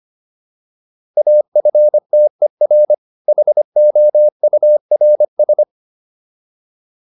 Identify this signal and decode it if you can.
Morse code, heard as a keyed tone, about 25 wpm, 600 Hz. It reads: AFTER HOURS